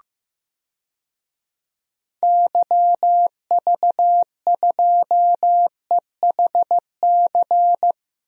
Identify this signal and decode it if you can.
Morse code, heard as a keyed tone, about 15 wpm, 700 Hz. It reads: YV2EHC
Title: DXCC-100 Call Signs - 6 Characters - 15wpm